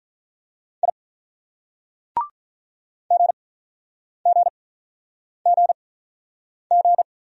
Morse code, heard as a keyed tone, about 44 wpm, spaced as for 9 wpm, 700 Hz. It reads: I GGGG